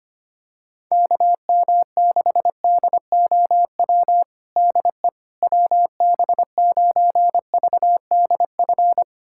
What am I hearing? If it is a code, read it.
Morse code, heard as a keyed tone, about 25 wpm, 700 Hz. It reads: KM6DOW DE WB9VDF
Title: USA Callsign DE Callsign - 6 Characters - 25wpm